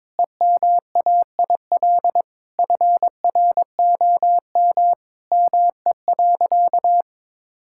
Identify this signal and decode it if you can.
Morse code, heard as a keyed tone, about 22 wpm, 700 Hz. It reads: EMAIL FROM ME.
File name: Sentences with 7-Word Phrases - 22wpm